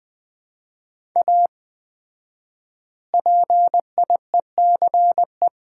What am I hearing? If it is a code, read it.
Morse code, heard as a keyed tone, about 20 wpm, 700 Hz. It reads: A PIECE
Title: 4x Word-Spacing - Sets of 3 Words - 20wpm